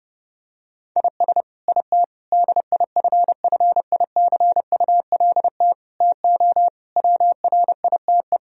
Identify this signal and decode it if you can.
Morse code, heard as a keyed tone, about 30 wpm, 700 Hz. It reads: IS IT DIFFICULT TO WRITE
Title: Sentences from Top 1000 Words - Review Part 3 - 30wpm